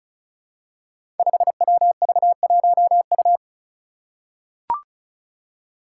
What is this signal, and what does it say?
Morse code, heard as a keyed tone, about 34 wpm, 700 Hz. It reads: 5WV1U
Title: Five Challenging Letter-Number Combinations 35wpm